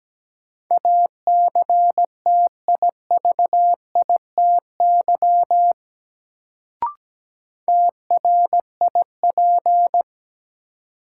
Morse code, heard as a keyed tone, about 17 wpm, 700 Hz. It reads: ACTIVITY TRIP
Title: Sentences from Top 1000 Words - Review Part 4 - New Word Drill - 17wpm